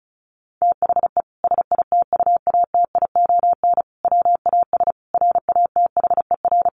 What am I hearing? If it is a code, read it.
Morse code, heard as a keyed tone, about 34 wpm, 700 Hz. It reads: THE SITUATION WAS RATHER
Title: Sentences from Top 1300 Words 35wpm